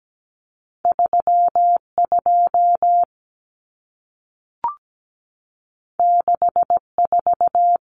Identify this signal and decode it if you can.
Morse code, heard as a keyed tone, about 17 wpm, 700 Hz. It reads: 32 64
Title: Common Cut Numbers - 2-Digit 17wpm